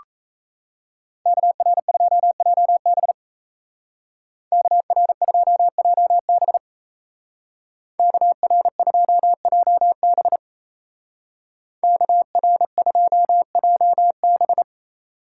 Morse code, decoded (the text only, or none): KR2JB KR2JB KR2JB KR2JB